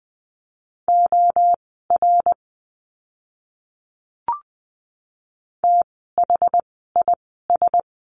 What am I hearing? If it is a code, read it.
Morse code, heard as a keyed tone, about 20 wpm, spaced as for 14 wpm, 700 Hz. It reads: OR THIS
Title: 068 Lesson 12f - Letter d - Words 20wpm